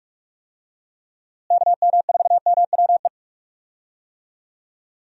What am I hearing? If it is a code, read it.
Morse code, heard as a keyed tone, about 42 wpm, 700 Hz. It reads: KM4MWE
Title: USA Call Signs - 6 Characters - Repeated 3x - 45wpm